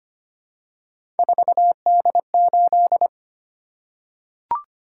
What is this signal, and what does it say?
Morse code, heard as a keyed tone, about 25 wpm, 700 Hz. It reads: 4D8
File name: Three Letter-Number Combinations - 25wpm